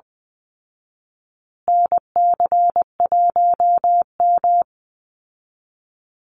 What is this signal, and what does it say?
Morse code, heard as a keyed tone, about 20 wpm, 700 Hz. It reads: NC1M
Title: Call Signs Repeated 2x - 20wpm